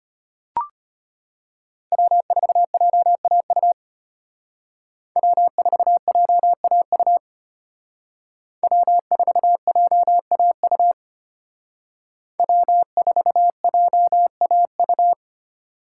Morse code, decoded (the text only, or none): W4JAU W4JAU W4JAU W4JAU